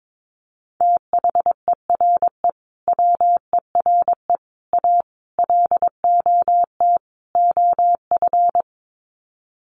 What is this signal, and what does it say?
Morse code, heard as a keyed tone, about 22 wpm, 700 Hz. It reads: THERE WERE A LOT OF
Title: Sets of 5 Words 22wpm